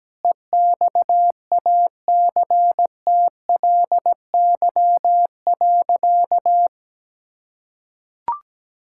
Morse code, heard as a keyed tone, about 17 wpm, 700 Hz. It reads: EXACTLY.
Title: Sentences from Top 1000 Words - Review Part 6 - 17wpm